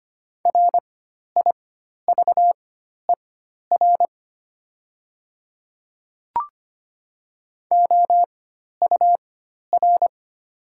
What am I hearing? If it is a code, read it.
Morse code, heard as a keyed tone, about 25 wpm, spaced as for 12 wpm, 700 Hz. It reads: RIVER OUR
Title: 4x Character Spacing - Top 300 Words - 25wpm